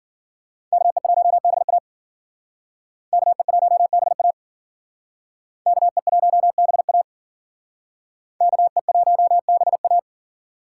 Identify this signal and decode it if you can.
Morse code, heard as a keyed tone, about 52 wpm, 700 Hz. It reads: KE1BA KE1BA KE1BA KE1BA